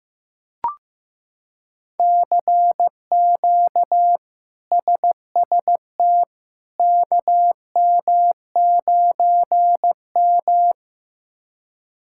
Morse code, heard as a keyed tone, about 15 wpm, 700 Hz. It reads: CQ SST KM9M
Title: K1USN SST QSO Practice - 15wpm